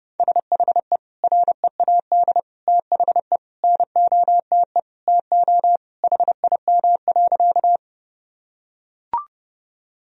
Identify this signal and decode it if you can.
Morse code, heard as a keyed tone, about 30 wpm, 700 Hz. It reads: SHE READ THE NOTE TO HIM.
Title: Sentences from Top 1000 Words - Review Part 2 - 30wpm